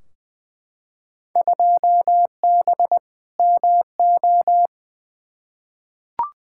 Morse code, heard as a keyed tone, about 20 wpm, 700 Hz. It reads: 2B MO
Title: Field Day Exchanges - 20wpm